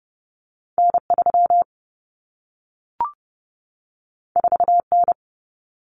Morse code, heard as a keyed tone, about 30 wpm, 700 Hz. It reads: N3 4N
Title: Common Cut Numbers - 2-Digit 30wpm